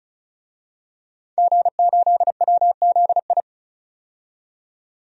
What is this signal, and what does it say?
Morse code, heard as a keyed tone, about 36 wpm, 700 Hz. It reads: G8WZI
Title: DXCC-100 Call Signs - 5 Characters - 35wpm